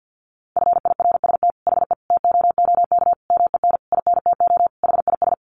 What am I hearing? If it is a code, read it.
Morse code, heard as a keyed tone, about 48 wpm, 700 Hz. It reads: FIRST HE TOOK MEN INTO HIS
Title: Sentences from Top 1500 Words 50wpm